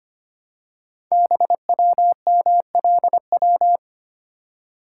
Morse code, heard as a keyed tone, about 25 wpm, 700 Hz. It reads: BWMLW
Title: Code Groups - 5 Letters - 25wpm